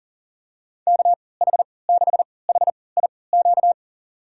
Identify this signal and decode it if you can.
Morse code, heard as a keyed tone, about 37 wpm, spaced as for 22 wpm, 700 Hz. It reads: KH6HIQ
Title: USA Call Signs - 6 Characters - 3x Character Spacing - 40wpm